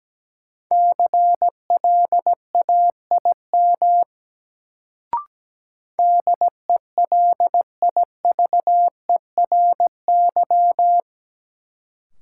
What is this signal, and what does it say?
Morse code, heard as a keyed tone, about 17 wpm, 700 Hz. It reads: CLAIM DELIVERY
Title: Sentences from Top 1500 Words - New Word Drill - 17wpm